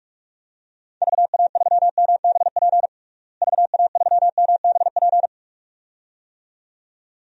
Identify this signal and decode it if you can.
Morse code, heard as a keyed tone, about 41 wpm, 700 Hz. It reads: VA3MBP VA3MBP
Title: DXCC-100 Call Signs - 6 Characters - Repeated 2x - 45wpm